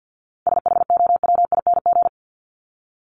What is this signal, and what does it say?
Morse code, heard as a keyed tone, about 49 wpm, 700 Hz. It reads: SHOWING
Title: ING Suffix - Top 100 Words with Base Words 50wpm